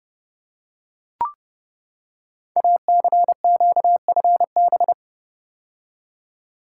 Morse code, heard as a keyed tone, about 30 wpm, 690 Hz. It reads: ACQFB
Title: Code Groups - 5 Letters - 30wpm